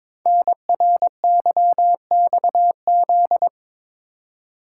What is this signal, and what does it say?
Morse code, heard as a keyed tone, about 22 wpm, 700 Hz. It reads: NRYXZ
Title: Code Groups - 5 Letters - 22wpm